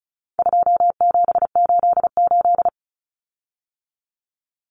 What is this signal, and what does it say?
Morse code, heard as a keyed tone, about 34 wpm, 700 Hz. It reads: 2788
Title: Common Cut Numbers - 4-Digit 35wpm